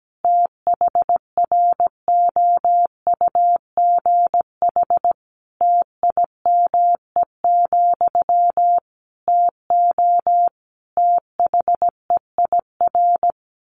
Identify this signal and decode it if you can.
Morse code, heard as a keyed tone, about 17 wpm, 700 Hz. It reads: THROUGH TIME, TO THEIR
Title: A Sound Of Thunder 17wpm